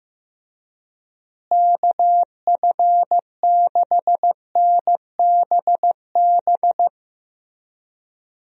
Morse code, heard as a keyed tone, about 15 wpm, 700 Hz. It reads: KF6NBB
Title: USA Call Signs - 6 Characters - Repeated 3x - 15wpm